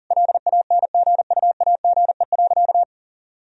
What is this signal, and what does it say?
Morse code, heard as a keyed tone, about 40 wpm, 700 Hz. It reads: LANGUAGE.
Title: Sentences from Top 1000 Words - Review Part 3 - 40wpm